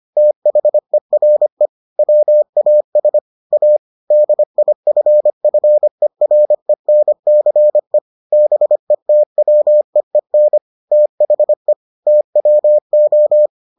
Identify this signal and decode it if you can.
Morse code, heard as a keyed tone, about 25 wpm, 600 Hz. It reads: THERE WAS A DIFFERENCE BETWEEN THE TWO